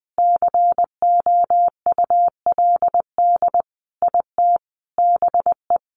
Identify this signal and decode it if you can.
Morse code, heard as a keyed tone, about 19 wpm, 700 Hz. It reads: COULD IT BE
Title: Sentences from Top 1400 Words 20wpm